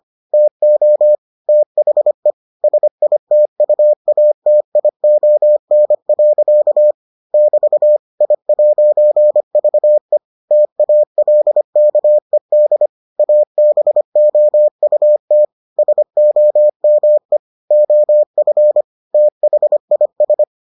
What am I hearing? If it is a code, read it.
Morse code, heard as a keyed tone, about 25 wpm, 600 Hz. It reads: TO THE SITUATION. <BT> I'VE TALKED ABOUT SOME OF THIS